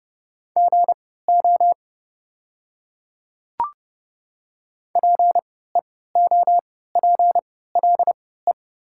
Morse code, heard as a keyed tone, about 31 wpm, spaced as for 17 wpm, 700 Hz. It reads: GO PEOPLE
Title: 3x Character Spacing - Top 100 Words - 30wpm